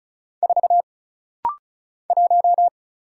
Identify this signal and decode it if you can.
Morse code, heard as a keyed tone, about 36 wpm, 700 Hz. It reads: 4 1
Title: Single Letter-Number - Rapid-Fire 35wpm